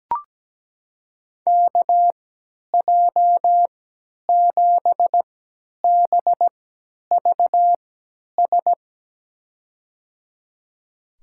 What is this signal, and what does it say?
Morse code, heard as a keyed tone, about 17 wpm, spaced as for 10 wpm, 700 Hz. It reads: KJ7BVS
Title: USA Call Signs - 6 Characters - 3x Character Spacing - 17wpm